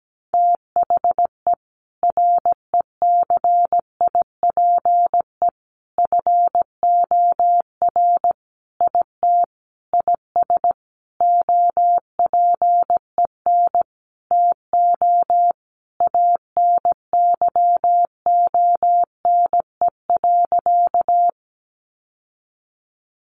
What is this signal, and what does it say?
Morse code, heard as a keyed tone, about 17 wpm, 700 Hz. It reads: THE RECIPE FOR IT IS OPEN TO ANYONE.
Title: Sentences from Top 1200 Words 17wpm